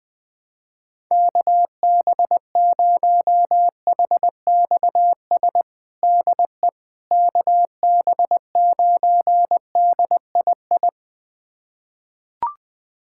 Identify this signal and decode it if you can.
Morse code, heard as a keyed tone, about 20 wpm, 700 Hz. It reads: KB0HXS DE KB9DII
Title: USA Callsign DE Callsign - 6 Characters - 20wpm